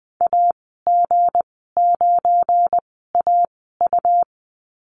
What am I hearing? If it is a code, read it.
Morse code, heard as a keyed tone, about 20 wpm, spaced as for 14 wpm, 700 Hz. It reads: AG9AU
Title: 186 Lesson 30g - Letter b - Call Signs 20wpm